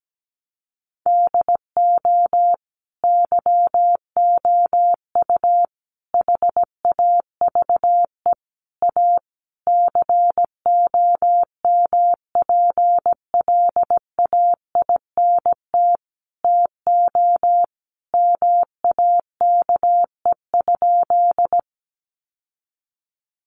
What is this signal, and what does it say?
Morse code, heard as a keyed tone, about 17 wpm, 700 Hz. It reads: DO YOU HAVE A COMPLAINT TO MAKE?
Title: Sentences from Top 1400 Words 17wpm